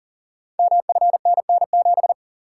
Morse code, heard as a keyed tone, about 39 wpm, 690 Hz. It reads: MFNN7
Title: Code Groups - 5 Letter:Numbers - 40wpm